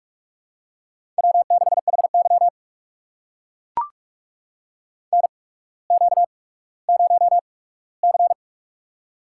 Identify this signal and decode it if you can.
Morse code, heard as a keyed tone, about 44 wpm, spaced as for 19 wpm, 690 Hz. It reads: W6HY NQ0C
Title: USA Call Signs - 4 Characters - 8x Character Spacing - 45wpm